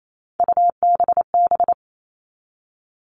Morse code, heard as a keyed tone, about 28 wpm, 700 Hz. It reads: UBB
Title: All Cut Numbers - 3-Digit 28wpm